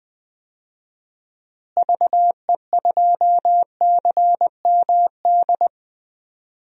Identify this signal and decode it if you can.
Morse code, heard as a keyed tone, about 20 wpm, 700 Hz. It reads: VE2CMD
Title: DXCC-100 Call Signs - 6 Characters - Repeated 3x - 20wpm